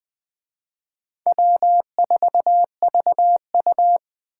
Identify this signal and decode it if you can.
Morse code, heard as a keyed tone, about 20 wpm, 700 Hz. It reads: W4VU